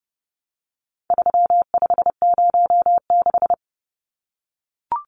Morse code, heard as a keyed tone, about 30 wpm, 700 Hz. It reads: 3506
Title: Numbers - 4-Digit 30wpm